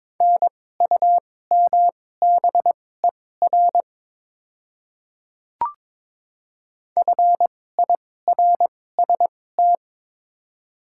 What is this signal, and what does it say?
Morse code, heard as a keyed tone, about 23 wpm, spaced as for 16 wpm, 700 Hz. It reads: NUMBER FIRST
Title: 2x Character Spacing - Top 300 Words - 22wpm